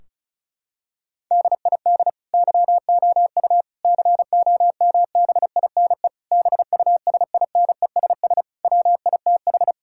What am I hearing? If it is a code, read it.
Morse code, heard as a keyed tone, about 34 wpm, 700 Hz. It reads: DID YOU COMBINE BUSINESS WITH
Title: Sentences from Top 1600 Words 35wpm